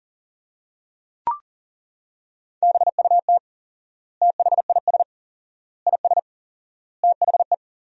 Morse code, heard as a keyed tone, about 40 wpm, 700 Hz. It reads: BUT THIS IS THE
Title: 4x Word-Spacing - Sentences from Top 200 Words - 40wpm